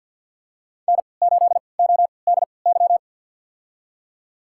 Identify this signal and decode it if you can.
Morse code, heard as a keyed tone, about 51 wpm, spaced as for 28 wpm, 700 Hz. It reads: N8ODY
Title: USA Call Signs - 5 Characters - 3x Character Spacing - 50wpm